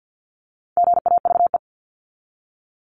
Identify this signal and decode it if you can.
Morse code, heard as a keyed tone, about 50 wpm, 700 Hz. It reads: GAVE
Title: Top 500 Words 50wpm